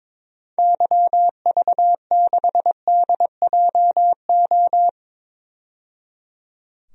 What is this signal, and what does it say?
Morse code, heard as a keyed tone, about 21 wpm, 700 Hz. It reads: YV6DJO
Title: DXCC-100 Call Signs - 6 Characters - 22wpm